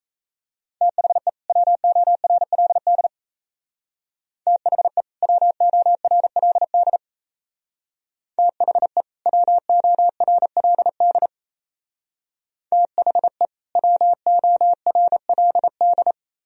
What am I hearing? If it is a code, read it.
Morse code, heard as a keyed tone, about 34 wpm, 700 Hz. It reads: THE WORLD THE WORLD THE WORLD THE WORLD
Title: Speed-Racing - Sets of 2 Words 28wpm